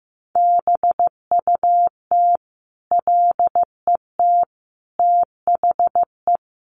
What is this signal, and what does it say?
Morse code, heard as a keyed tone, about 15 wpm, 700 Hz. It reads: BUT LET THE
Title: Sentences from Top 1300 Words 15wpm